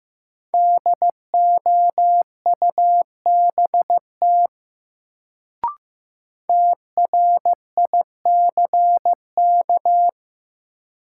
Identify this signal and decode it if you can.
Morse code, heard as a keyed tone, about 15 wpm, 700 Hz. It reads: DOUBT TRICK
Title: Sentences from Top 1500 Words - New Word Drill - 15wpm